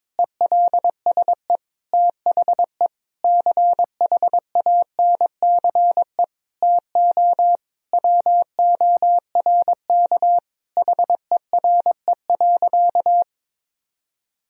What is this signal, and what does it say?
Morse code, heard as a keyed tone, about 22 wpm, 700 Hz. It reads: ELSE THE CHANCE TO WORK HERE.
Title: Sentences from Top 1000 Words - Review Part 4 - 22wpm